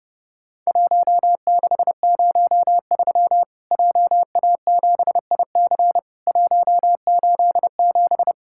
Morse code, heard as a keyed tone, about 30 wpm, 700 Hz. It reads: 1603 JA7IC 187
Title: Worked All Europe DX Contest - QTC Exchange - 30wpm